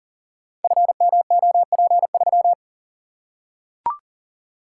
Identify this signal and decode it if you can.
Morse code, heard as a keyed tone, about 41 wpm, 700 Hz. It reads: FMOP3
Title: Code Groups - 5 Letter:Numbers - 40wpm